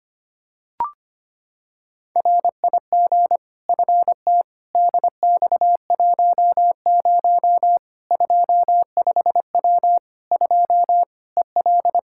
Morse code, heard as a keyed tone, about 25 wpm, 700 Hz. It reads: RIG FT DX10 25W 2 EL
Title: Practice QSO - Exchange 2 - Info - 25wpm